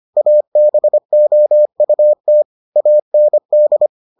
Morse code, heard as a keyed tone, about 25 wpm, 600 Hz. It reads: ABOUT AND